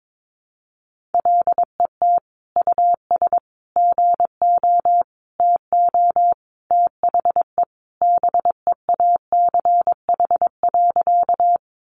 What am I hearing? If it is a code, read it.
Morse code, heard as a keyed tone, about 22 wpm, 700 Hz. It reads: LET US GO TO THE BEACH.
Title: Sentences from Top 1500 Words 22wpm